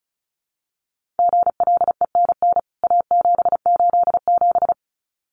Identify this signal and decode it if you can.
Morse code, heard as a keyed tone, about 34 wpm, 700 Hz. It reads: GLENN A787
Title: CWT Exchange 35wpm